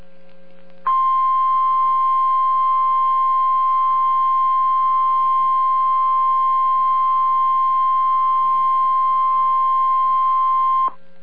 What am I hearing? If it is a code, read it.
Emergency Alert System (SAME) tones (the 1050 Hz attention tone). It is heard from 0.8 to 10.9 s.